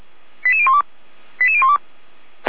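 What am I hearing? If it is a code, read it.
Five-tone selective call (ZVEI), heard as quick runs of five tones, tones of about 70 ms. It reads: 88012 88012
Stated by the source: Five-tone ZVEI selective call received by SDR, from audio.f85360.feuerwehr-orlamuende.org, the Orlamünde fire brigade.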